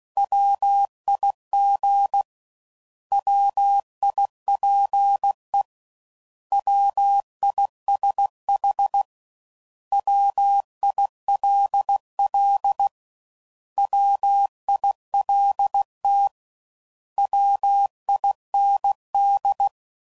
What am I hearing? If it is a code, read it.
Morse code, heard as a keyed tone, about 16 wpm, 800 Hz. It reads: WIG WIPE WISH WILL WILT WIND
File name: Prefixes / By First Letter / Ws